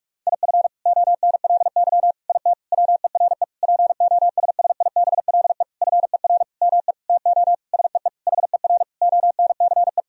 Morse code, heard as a keyed tone, about 46 wpm, 700 Hz. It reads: IF ONLY IT WERE POSSIBLE FER ME TO SEE HER ONCE
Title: Sentences from Top 500 Words and Abbreviations 45wpm